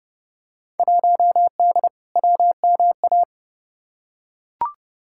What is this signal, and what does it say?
Morse code, heard as a keyed tone, about 30 wpm, 700 Hz. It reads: 1D WMA
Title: Field Day Exchanges - 30wpm